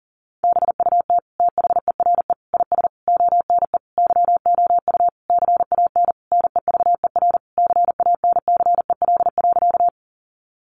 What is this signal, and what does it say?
Morse code, heard as a keyed tone, about 37 wpm, 700 Hz. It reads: BUT THERE IS ONE YOU CAN NEVER CANCEL.
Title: Sentences from Top 1200 Words 40wpm